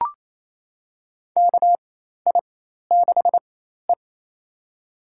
Morse code, heard as a keyed tone, about 28 wpm, spaced as for 13 wpm, 700 Hz. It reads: KI6E
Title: USA Call Signs - 4 Characters - 4x Character Spacing - 28wpm